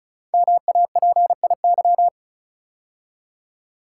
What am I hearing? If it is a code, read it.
Morse code, heard as a keyed tone, about 36 wpm, 690 Hz. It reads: MAPIY
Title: Code Groups - 5 Letters - 35wpm